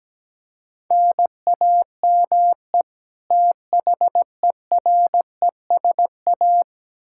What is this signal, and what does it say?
Morse code, heard as a keyed tone, about 17 wpm, 700 Hz. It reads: NAME THERESA
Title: K1USN SST QSO Practice 17wpm